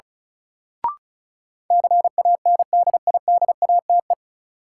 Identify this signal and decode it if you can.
Morse code, heard as a keyed tone, about 34 wpm, 700 Hz. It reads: CANDIDATE
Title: Sentences from Top 1600 Words - New Word Drill - 35wpm